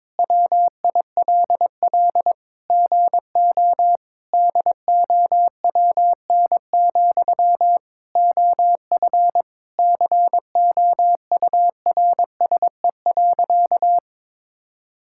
Morse code, heard as a keyed tone, about 22 wpm, 700 Hz. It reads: WILL GO DOWN, OF COURSE.
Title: Sentences from Top 1000 Words - Review Part 5 - 22wpm